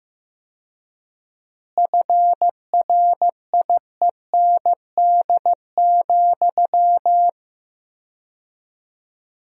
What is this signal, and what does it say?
Morse code, heard as a keyed tone, about 15 wpm, 700 Hz. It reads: FRIEND,
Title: 4x Word-Spacing - Sentences from Top 500 Words - 15wpm